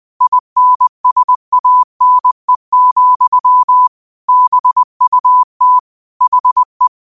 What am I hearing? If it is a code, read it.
Morse code, heard as a keyed tone, about 20 wpm, 1000 Hz. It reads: INSANE, BUT HE